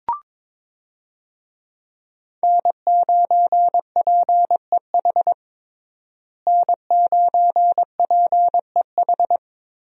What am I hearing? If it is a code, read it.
Morse code, heard as a keyed tone, about 22 wpm, 700 Hz. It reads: N9PEH N9PEH
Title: USA Call Signs - 5 Characters - Repeated 2x - 22wpm